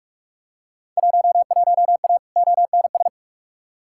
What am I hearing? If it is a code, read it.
Morse code, heard as a keyed tone, about 42 wpm, 700 Hz. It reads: 11A ONS
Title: Field Day Exchanges Encore - 45wpm